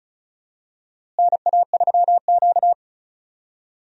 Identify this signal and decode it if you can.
Morse code, heard as a keyed tone, about 35 wpm, 700 Hz. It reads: NA3Q
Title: USA Call Signs - 4 Characters - Repeated 2x - 35wpm